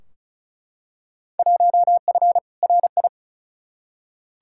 Morse code, heard as a keyed tone, about 36 wpm, 700 Hz. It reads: 1F RI
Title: Field Day Exchanges Encore - 35wpm